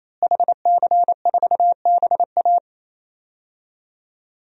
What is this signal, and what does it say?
Morse code, heard as a keyed tone, about 28 wpm, 700 Hz. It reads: HC4BA
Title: DXCC-100 Call Signs - 5 Characters - Repeated 3x - 28wpm